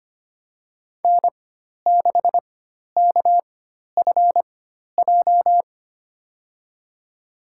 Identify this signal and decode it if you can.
Morse code, heard as a keyed tone, about 25 wpm, spaced as for 12 wpm, 700 Hz. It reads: N6KFJ
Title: USA Call Signs - 5 Characters - 4x Character Spacing - 25wpm